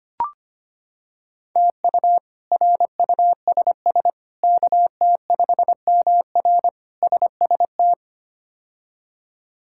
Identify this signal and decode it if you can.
Morse code, heard as a keyed tone, about 25 wpm, 700 Hz. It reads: TU RUSS KT5MR SST
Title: K1USN SST QSO Practice - 25wpm